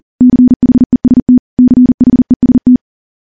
Morse code, heard as a keyed tone, about 43 wpm, 250 Hz. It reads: CHEST CHEST
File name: morse / morse-training-40WPM